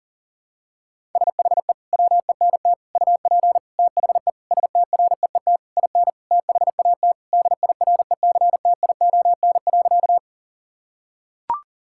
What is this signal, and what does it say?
Morse code, heard as a keyed tone, about 40 wpm, 700 Hz. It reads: SHE WENT UP THE STREET IN THAT DIRECTION.
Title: Sentences from Top 1000 Words - Review Part 5 - 40wpm